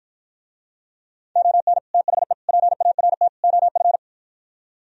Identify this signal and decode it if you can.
Morse code, heard as a keyed tone, about 52 wpm, 700 Hz. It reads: ON THE PART OF